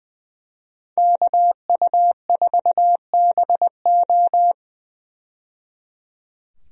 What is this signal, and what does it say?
Morse code, heard as a keyed tone, about 20 wpm, 700 Hz. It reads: KU4BO
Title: USA Call Signs - 5 Characters - 20wpm